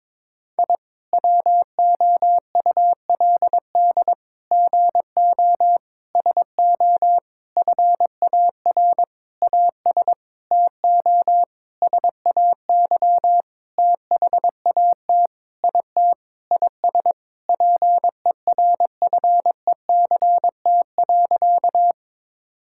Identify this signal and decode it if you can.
Morse code, heard as a keyed tone, about 22 wpm, 700 Hz. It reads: I WOULD GO SO FAR AS TO SAY THAT IT IS PERFECT.
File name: Sentences with 9-Word Phrases - 22wpm